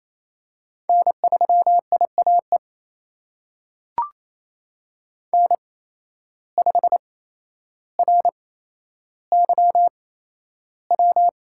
Morse code, heard as a keyed tone, about 28 wpm, spaced as for 12 wpm, 700 Hz. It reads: N3IAE N5RYW
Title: USA Call Signs - 5 Characters - 8x Character Spacing - 28wpm